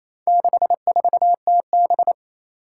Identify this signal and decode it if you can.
Morse code, heard as a keyed tone, about 28 wpm, 700 Hz. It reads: 64TB
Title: Four Challenging Letter-Number Combinations 28wpm